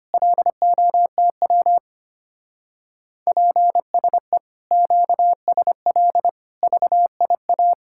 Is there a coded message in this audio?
Yes: Morse code from 0.1 to 7.7 s, heard as a keyed tone, about 32 wpm, 700 Hz. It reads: LOTW PSE QSL VIA